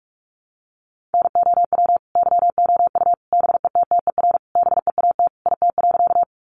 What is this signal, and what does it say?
Morse code, heard as a keyed tone, about 44 wpm, 700 Hz. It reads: NOW YOU BETTER BEAT IT.
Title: Sentences from Top 1300 Words 45wpm